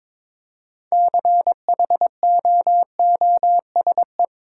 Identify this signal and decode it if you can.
Morse code, heard as a keyed tone, about 22 wpm, 700 Hz. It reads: CHOOSE
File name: practice / 22wpm / Sentences with 2-Word Phrases - 22wpm